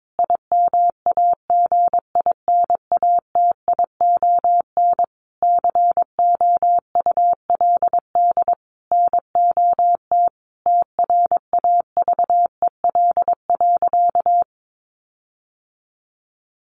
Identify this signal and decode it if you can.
Morse code, heard as a keyed tone, about 22 wpm, 700 Hz. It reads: IMAGINATION COULD NOT TRAVEL.
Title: Sentences from Top 1300 Words 22wpm